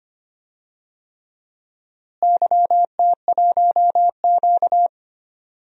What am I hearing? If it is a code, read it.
Morse code, heard as a keyed tone, about 25 wpm, 700 Hz. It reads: YT1Q
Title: DXCC-100 Call Signs - 4 Characters - Repeated 3x - 25wpm